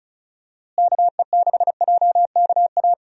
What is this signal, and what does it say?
Morse code, heard as a keyed tone, about 36 wpm, 700 Hz. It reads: KE6JKA
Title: USA Call Signs - 6 Characters - Repeated 2x - 35wpm